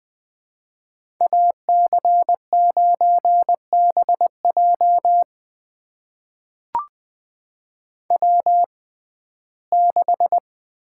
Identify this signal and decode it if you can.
Morse code, heard as a keyed tone, about 20 wpm, spaced as for 15 wpm, 700 Hz. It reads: AC9BJ W6
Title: USA Call Signs - 5 Characters - 6x Character Spacing - 20wpm